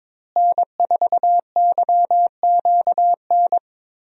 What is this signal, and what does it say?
Morse code, heard as a keyed tone, about 21 wpm, 700 Hz. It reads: N4YQN
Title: USA Call Signs - 5 Characters - Repeated 3x - 22wpm